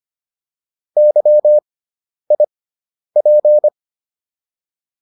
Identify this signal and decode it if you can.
Morse code, heard as a keyed tone, about 25 wpm, spaced as for 10 wpm, 600 Hz. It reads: YIP